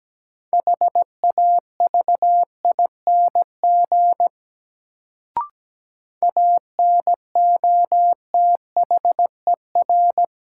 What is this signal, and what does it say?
Morse code, heard as a keyed tone, about 17 wpm, 700 Hz. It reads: HAVING ANOTHER
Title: Sets of 5 Words Encore - New Word Drill - 17wpm